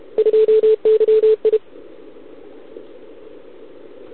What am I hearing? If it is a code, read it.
Morse code, heard as a keyed tone, about 31 wpm, 440 Hz. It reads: 2YI
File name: ARCHIV / OKDX2020 / OK1RIb